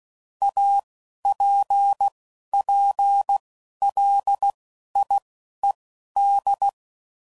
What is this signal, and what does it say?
Morse code, heard as a keyed tone, about 16 wpm, spaced as for 11 wpm, 800 Hz. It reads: APPLIED
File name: CW Code Groups @10WPM / Technical Subjects / XL